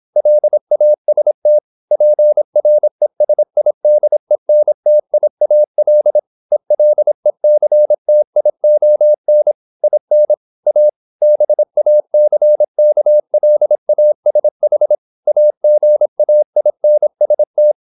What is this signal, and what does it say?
Morse code, heard as a keyed tone, about 26 wpm, 600 Hz. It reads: LAST PRESIDENTIAL ELECTION IN A BACKLASH AGAINST
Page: Never 26wpm | CW med Gnister